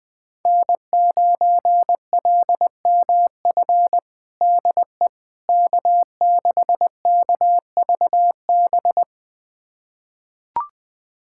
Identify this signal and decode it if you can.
Morse code, heard as a keyed tone, about 20 wpm, 700 Hz. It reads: N9LMF DE K6KVB
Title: USA Callsign DE Callsign - 5 Characters - 20wpm